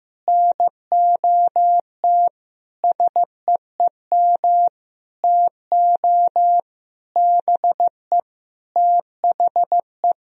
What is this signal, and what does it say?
Morse code, heard as a keyed tone, about 15 wpm, 700 Hz. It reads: NOT SEEM TO BE THE
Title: Sentences from Top 1000 Words - Review Part 2 - 15wpm